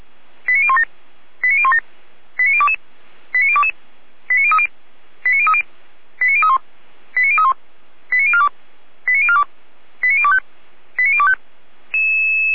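Five-tone selective call (ZVEI), heard as quick runs of five tones, tones of about 70 ms. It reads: 89017 89017 89022 89022 89030 89030 89031 89031 89042 89042 89025 89025
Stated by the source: ZVEI five-tone call received on SDR.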